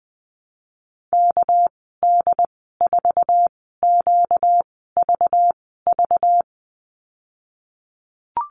Morse code, heard as a keyed tone, about 21 wpm, spaced as for 15 wpm, 700 Hz. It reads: KD4QVV
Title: 238 Lesson 38g - Letter q - Call Signs 20wpm